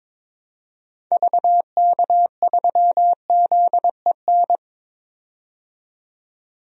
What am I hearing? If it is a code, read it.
Morse code, heard as a keyed tone, about 21 wpm, 700 Hz. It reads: VK3ZEN
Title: DXCC-100 Call Signs - 6 Characters - 22wpm